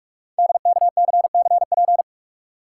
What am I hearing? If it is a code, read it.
Morse code, heard as a keyed tone, about 44 wpm, 700 Hz. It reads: DKCCP
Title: Code Groups - 5 Letters - 45wpm